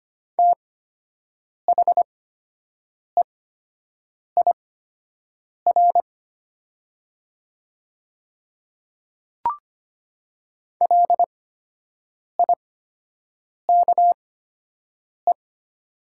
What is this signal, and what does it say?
Morse code, heard as a keyed tone, about 25 wpm, spaced as for 7 wpm, 700 Hz. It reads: THEIR LIKE
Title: 8x Character Spacing - Top 100 Words - 25wpm